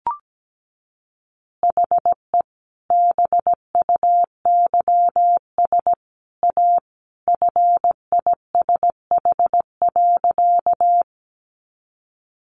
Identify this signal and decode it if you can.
Morse code, heard as a keyed tone, about 17 wpm, 700 Hz. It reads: HE BUYS A FISH.
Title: Additional Easy Phrases 17wpm